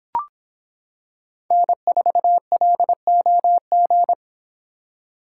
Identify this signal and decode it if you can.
Morse code, heard as a keyed tone, about 25 wpm, 700 Hz. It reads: N4LOG